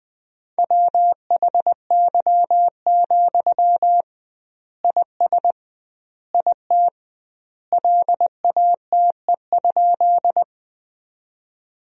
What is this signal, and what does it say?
Morse code, heard as a keyed tone, about 20 wpm, 700 Hz. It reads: WHY, IS IT LATE?
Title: 2x Word-Spacing - Sentences from Top 300 Words - 20wpm